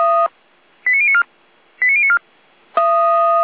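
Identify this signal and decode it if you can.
Five-tone selective call (ZVEI), heard as quick runs of five tones, tones of about 70 ms. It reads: 89094 89094
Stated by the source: SDR-received ZVEI five-tone selective call, code 89120.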